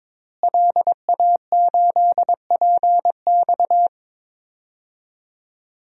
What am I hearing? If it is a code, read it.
Morse code, heard as a keyed tone, about 22 wpm, 700 Hz. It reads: LA8PX
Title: DXCC-100 Call Signs - 5 Characters - Repeated 3x - 22wpm